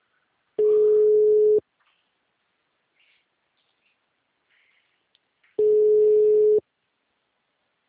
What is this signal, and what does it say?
Ringback tone (Europe, 425 Hz).
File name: out- -2028- - - .45